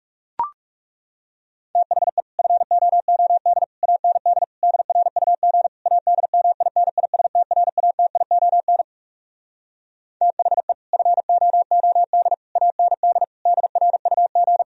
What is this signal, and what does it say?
Morse code, heard as a keyed tone, about 40 wpm, 700 Hz. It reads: THE FOOD AND DRUG ADMINISTRATION THE FOOD AND DRUG